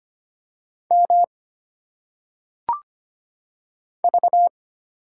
Morse code, heard as a keyed tone, about 25 wpm, 700 Hz. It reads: M V
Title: Single Letters - Phonetic Pronunciation 25wpm